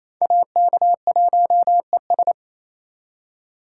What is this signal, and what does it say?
Morse code, heard as a keyed tone, about 36 wpm, 700 Hz. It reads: AK1ES